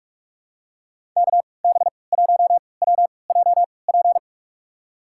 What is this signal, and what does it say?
Morse code, heard as a keyed tone, about 46 wpm, spaced as for 26 wpm, 690 Hz. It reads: KB1WJP
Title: USA Call Signs - 6 Characters - 3x Character Spacing - 45wpm